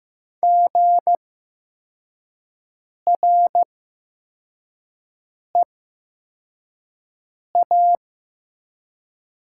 Morse code, heard as a keyed tone, about 15 wpm, spaced as for 4 wpm, 700 Hz. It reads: GREA
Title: 8x Character Spacing - Top 200 Words - 15wpm